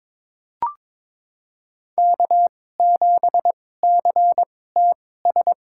Morse code, heard as a keyed tone, about 22 wpm, spaced as for 16 wpm, 700 Hz. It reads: K7CTS
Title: USA Call Signs - 5 Characters - 2x Character Spacing - 22wpm